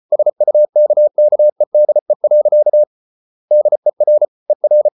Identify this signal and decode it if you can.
Morse code, heard as a keyed tone, about 34 wpm, 600 Hz. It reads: SUKKEDE. DER ER
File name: Ferie QRQ 34wpm